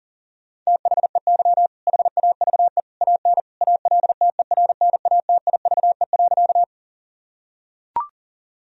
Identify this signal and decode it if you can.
Morse code, heard as a keyed tone, about 40 wpm, 700 Hz. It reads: THEY HAVE AN ALTERNATIVE.
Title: Sentences from Top 1000 Words - Review Part 4 - 40wpm